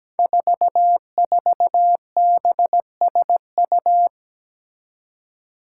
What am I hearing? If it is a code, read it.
Morse code, heard as a keyed tone, about 17 wpm, 700 Hz. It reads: 44BSU
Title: Five Challenging Letter-Number Combinations 17wpm